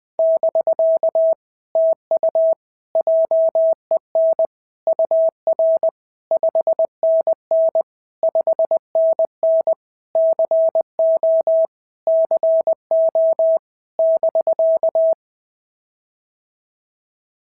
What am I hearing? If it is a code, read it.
Morse code, heard as a keyed tone, about 20 wpm, 650 Hz. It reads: <BK> TU JEN UR 5NN 5NN CO CO <BK>
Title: POTA QSO Practice - 20wpm